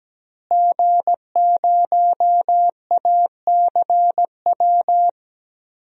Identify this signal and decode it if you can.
Morse code, heard as a keyed tone, about 17 wpm, 700 Hz. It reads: G0ACW
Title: DXCC-100 Call Signs - 5 Characters - Repeated 3x - 17wpm